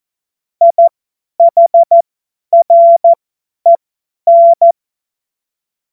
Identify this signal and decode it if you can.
Morse code, heard as a keyed tone, about 14 wpm, spaced as for 10 wpm, 680 Hz. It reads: IHREN
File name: Morsekurs / Witze / Witz 22. Juni 2021, 50 F70